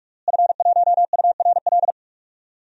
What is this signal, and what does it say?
Morse code, heard as a keyed tone, about 46 wpm, 700 Hz. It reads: F1URL
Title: DXCC-100 Call Signs - 5 Characters - Repeated 3x - 45wpm